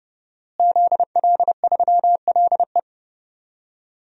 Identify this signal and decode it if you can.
Morse code, heard as a keyed tone, about 32 wpm, 700 Hz. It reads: ZL3LE
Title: DXCC-100 Call Signs - 5 Characters - Repeated 2x - 30wpm